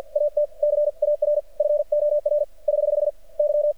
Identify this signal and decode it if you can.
Morse code, heard as a keyed tone, about 50 wpm, 590 Hz. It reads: AT CAR FOR <HH> Y
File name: archived / CRSnet